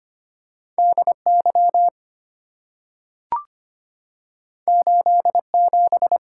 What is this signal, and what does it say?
Morse code, heard as a keyed tone, about 25 wpm, 700 Hz. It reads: DY 87
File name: Two Letter-Number Combinations - 25wpm